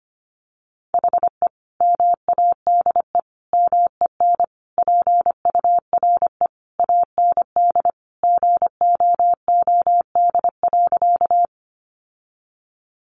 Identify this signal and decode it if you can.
Morse code, heard as a keyed tone, about 25 wpm, 700 Hz. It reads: HE MADE MEN PURE AND GOOD.
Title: Sentences from Top 1500 Words 25wpm